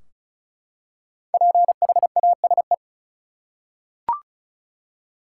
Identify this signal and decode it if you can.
Morse code, heard as a keyed tone, about 36 wpm, 700 Hz. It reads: PHASE
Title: Five Letter Words - 35wpm